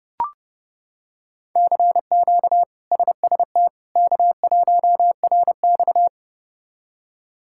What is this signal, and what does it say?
Morse code, heard as a keyed tone, about 30 wpm, 700 Hz. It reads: CQ SST K1RX
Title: K1USN SST QSO Practice - 30wpm